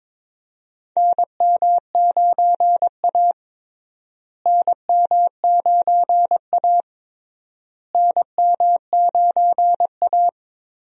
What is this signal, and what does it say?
Morse code, heard as a keyed tone, about 22 wpm, 700 Hz. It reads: NM9A NM9A NM9A
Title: USA Call Signs - 4 Characters - Repeated 3x - 22wpm